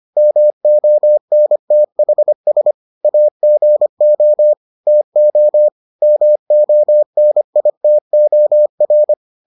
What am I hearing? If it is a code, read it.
Morse code, heard as a keyed tone, about 25 wpm, 600 Hz. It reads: MONTHS AGO TO MONITOR